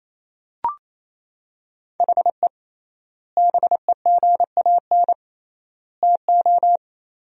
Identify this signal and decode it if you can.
Morse code, heard as a keyed tone, about 28 wpm, 700 Hz. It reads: HE BEGAN TO
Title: 3x Word-Spacing - Sentences from Top 500 Words - 28wpm